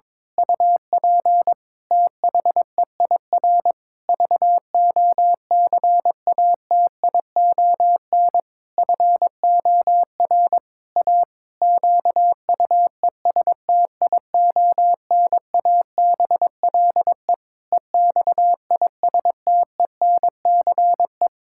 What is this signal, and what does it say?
Morse code, heard as a keyed tone, about 22 wpm, 700 Hz. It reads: UP THEIR VOCATION FOR A QUESTIONABLE EXISTENCE
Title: Pangram Challenge v2 - 22wpm